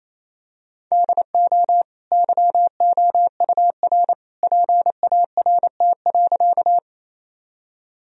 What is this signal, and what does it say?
Morse code, heard as a keyed tone, about 28 wpm, 700 Hz. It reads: DO YOUR PART.
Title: Sentences from Top 1000 Words - Review Part 1 - 28wpm